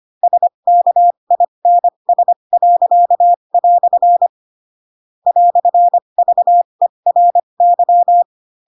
Morse code, heard as a keyed tone, about 25 wpm, spaced as for 22 wpm, 700 Hz. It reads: SKINS." "VERY
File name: Pinocchio / Pinocchio ch7 18 25 ik1hge